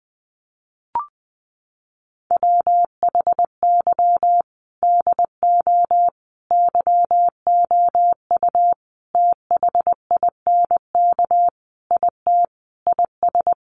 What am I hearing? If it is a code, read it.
Morse code, heard as a keyed tone, about 20 wpm, 700 Hz. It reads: WHY DO YOU THINK IT IS
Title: Sentences from Top 1400 Words 20wpm